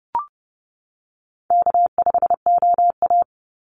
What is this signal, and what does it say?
Morse code, heard as a keyed tone, about 30 wpm, 700 Hz. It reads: K5OA
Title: Call Signs - 30wpm